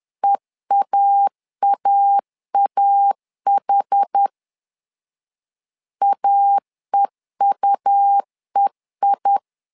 Morse code, heard as a keyed tone, about 10 wpm, 800 Hz. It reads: EAAAH AEUEI
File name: LEZIONI / 2 UHDN / 2 UH 10wpm 1714